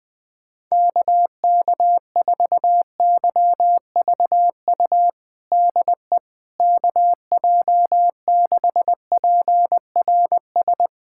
Morse code, heard as a keyed tone, about 20 wpm, 700 Hz. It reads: KK4YVU DE KJ6PRS
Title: USA Callsign DE Callsign 20wpm